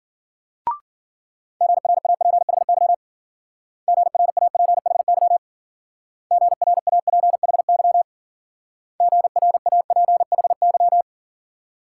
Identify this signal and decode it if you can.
Morse code, heard as a keyed tone, about 53 wpm, 700 Hz. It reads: GRAPHY GRAPHY GRAPHY GRAPHY